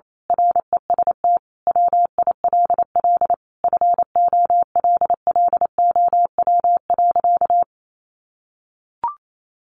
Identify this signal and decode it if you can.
Morse code, heard as a keyed tone, about 28 wpm, 700 Hz. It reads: REST WILL FOLLOW.
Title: Sentences from Top 1500 Words 28wpm